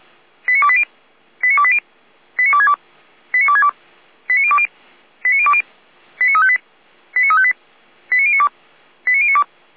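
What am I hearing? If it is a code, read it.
Five-tone selective call (ZVEI), heard as quick runs of five tones, tones of about 70 ms. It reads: 89280 89280 89262 89262 89020 89020 89358 89358 89092 89092
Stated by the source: ZVEI five-tone call 89304 received by SDR.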